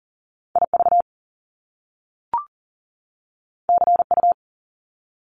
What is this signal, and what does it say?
Morse code, heard as a keyed tone, about 38 wpm, 700 Hz. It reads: IV CU
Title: Top Two Letter Combinations in Words 40wpm